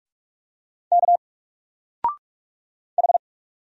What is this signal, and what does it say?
Morse code, heard as a keyed tone, about 44 wpm, 700 Hz. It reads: K H
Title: Single Letter-Number 45wpm